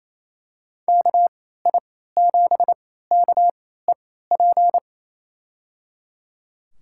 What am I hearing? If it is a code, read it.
Morse code, heard as a keyed tone, about 28 wpm, spaced as for 16 wpm, 700 Hz. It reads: KI7KEP
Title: USA Call Signs - 6 Characters - 3x Character Spacing - 28wpm